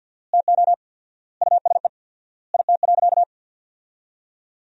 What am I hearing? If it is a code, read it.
Morse code, heard as a keyed tone, about 48 wpm, 700 Hz. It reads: TO USE IT.
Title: 4x Word-Spacing - Sentences from Top 500 Words - 50wpm